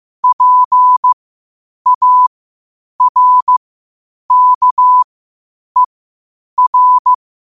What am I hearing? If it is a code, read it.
Morse code, heard as a keyed tone, about 15 wpm, spaced as for 8 wpm, 1000 Hz. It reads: PARKER
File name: qotdfiles / 8wpm